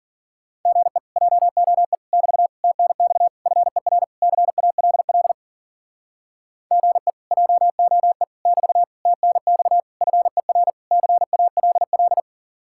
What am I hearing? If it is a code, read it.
Morse code, heard as a keyed tone, about 50 wpm, 700 Hz. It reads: GE JOE <BT> TNX FER CALL GE JOE <BT> TNX FER CALL